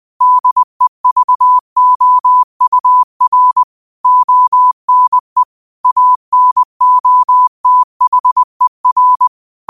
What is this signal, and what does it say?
Morse code, heard as a keyed tone, about 20 wpm, 1000 Hz. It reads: DEVOUR ONE ANOTHER